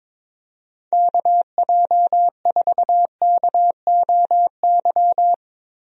Morse code, heard as a keyed tone, about 22 wpm, 700 Hz. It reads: KJ4KOY
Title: USA Call Signs - 6 Characters - Repeated 3x - 22wpm